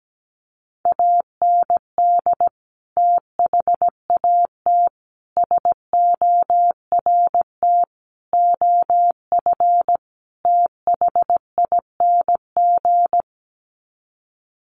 Morse code, heard as a keyed tone, about 17 wpm, 700 Hz. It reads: AND THAT SORT OF THING
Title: Sets of 5 Words Encore2 17wpm